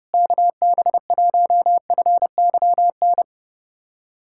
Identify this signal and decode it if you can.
Morse code, heard as a keyed tone, about 32 wpm, 700 Hz. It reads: KB1FYN
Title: USA Callsign DE Callsign - 6 Characters - 30wpm